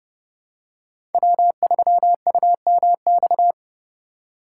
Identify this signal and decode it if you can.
Morse code, heard as a keyed tone, about 32 wpm, 700 Hz. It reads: W3UMX
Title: USA Call Signs - 5 Characters - 3x Character Spacing - 30wpm